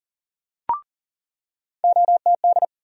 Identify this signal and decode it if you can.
Morse code, heard as a keyed tone, about 40 wpm, 700 Hz. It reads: OTD
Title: USA Call Sign Postfixes - 3 Letters 40wpm